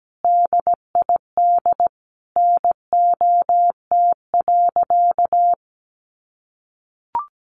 Morse code, heard as a keyed tone, about 17 wpm, 700 Hz. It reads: DID NOT.
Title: Sentences from Top 700 Words 17wpm